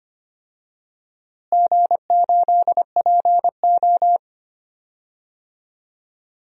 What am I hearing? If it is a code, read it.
Morse code, heard as a keyed tone, about 25 wpm, 700 Hz. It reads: G8PO
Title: DXCC-100 Call Signs - 4 Characters - Repeated 3x - 25wpm